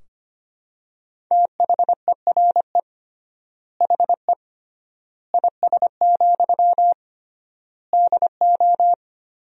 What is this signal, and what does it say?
Morse code, heard as a keyed tone, about 25 wpm, 700 Hz. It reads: THERE HE IS, DO
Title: 3x Word-Spacing - Sentences from Top 100 Words - 25wpm